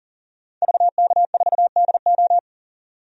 Morse code, heard as a keyed tone, about 40 wpm, 700 Hz. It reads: VK4DO
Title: DXCC-100 Call Signs - 5 Characters - 40wpm